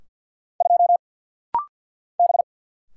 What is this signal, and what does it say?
Morse code, heard as a keyed tone, about 50 wpm, 700 Hz. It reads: 2 B
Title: Single Letter-Number Alt - Rapid-Fire 50wpm